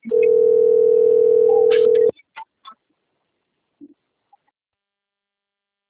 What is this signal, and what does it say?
Ringback tone (North America), 440 + 480 Hz.
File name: rg-1600-unknown- - - .25780